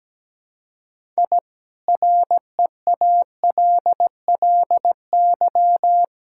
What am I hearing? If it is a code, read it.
Morse code, heard as a keyed tone, about 17 wpm, 700 Hz. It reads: I REALLY
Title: Sentences from Top 1000 Words - Review Part 6 - 17wpm